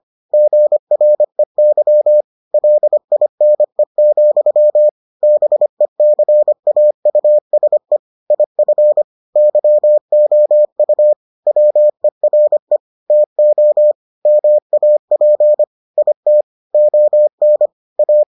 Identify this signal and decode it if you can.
Morse code, heard as a keyed tone, about 25 wpm, 600 Hz. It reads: GREY LINE, BECAUSE IF YOU WERE TO MAP IT ON A